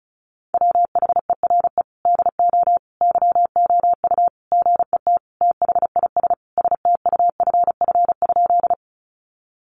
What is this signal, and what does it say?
Morse code, heard as a keyed tone, about 34 wpm, 700 Hz. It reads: WHERE DO YOU GET THIS STUFF?
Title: Sentences from Top 1400 Words 35wpm